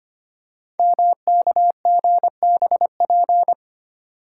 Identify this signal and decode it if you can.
Morse code, heard as a keyed tone, about 25 wpm, 700 Hz. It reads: MKGBP
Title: Code Groups - 5 Letters - 25wpm